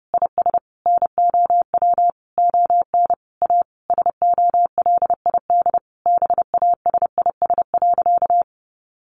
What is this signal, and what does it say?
Morse code, heard as a keyed tone, about 30 wpm, 700 Hz. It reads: IS NOW ON A SOLID BASIS.
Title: Sentences from Top 1500 Words 30wpm